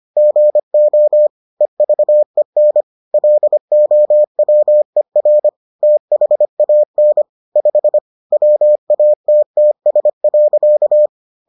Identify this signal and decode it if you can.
Morse code, heard as a keyed tone, about 25 wpm, 600 Hz. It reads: GO EVEN LOWER THAN 5 WATTS.